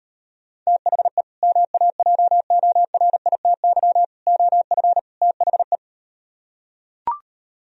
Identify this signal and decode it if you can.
Morse code, heard as a keyed tone, about 38 wpm, 690 Hz. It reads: THE MAJORITY OF THE